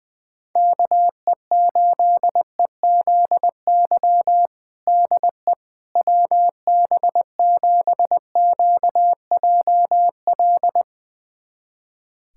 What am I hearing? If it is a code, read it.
Morse code, heard as a keyed tone, about 20 wpm, 700 Hz. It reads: KE8EZY DE WB7QJL
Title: USA Callsign DE Callsign - 6 Characters - 20wpm